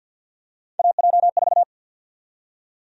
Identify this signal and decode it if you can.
Morse code, heard as a keyed tone, about 48 wpm, 700 Hz. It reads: AJ4
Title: USA Call Sign Prefixes - 50wpm